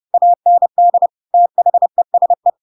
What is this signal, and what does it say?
Morse code, heard as a keyed tone, about 30 wpm, 700 Hz. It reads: AND THESE
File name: Pinocchio / Pinocchio ch7 30 30 ik1hge